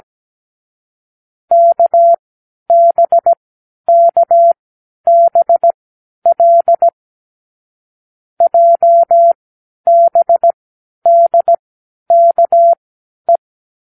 Morse code, heard as a keyed tone, about 17 wpm, spaced as for 10 wpm, 680 Hz. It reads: KBKBL JBDKE
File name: k7qo / Lesson18(A-L)